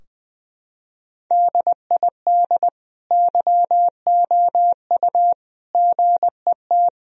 Morse code, heard as a keyed tone, about 20 wpm, 700 Hz. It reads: DID YOU GET
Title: Sentences from Top 1000 Words - Review Part 2 - 20wpm